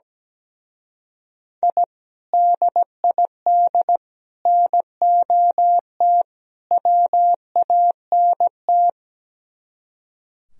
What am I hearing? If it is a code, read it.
Morse code, heard as a keyed tone, about 17 wpm, 700 Hz. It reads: I DID NOT WANT
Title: Speed-Racing - Sets of 4 Words 17wpm